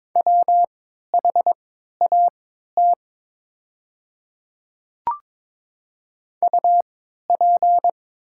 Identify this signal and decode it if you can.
Morse code, heard as a keyed tone, about 22 wpm, spaced as for 12 wpm, 700 Hz. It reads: WHAT UP
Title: 3x Character Spacing - Top 100 Words - 22wpm